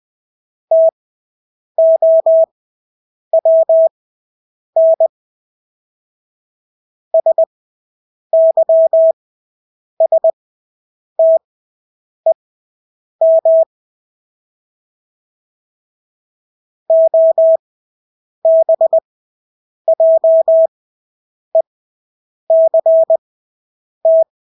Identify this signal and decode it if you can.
Morse code, heard as a keyed tone, about 20 wpm, spaced as for 8 wpm, 650 Hz. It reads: TOWN SYSTEM OBJECT
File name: Engelske ord 20/8 wpm